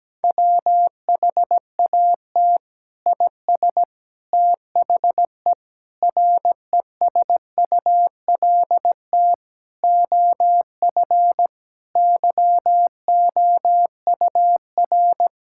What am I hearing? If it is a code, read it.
Morse code, heard as a keyed tone, about 17 wpm, 700 Hz. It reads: WHAT IS THE RESULT OF YOUR
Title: Sentences from Top 1000 Words - Review Part 5 - 17wpm